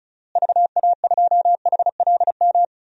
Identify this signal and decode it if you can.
Morse code, heard as a keyed tone, about 36 wpm, 700 Hz. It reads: VA2HLM
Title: DXCC-100 Call Signs - 6 Characters - Repeated 3x - 35wpm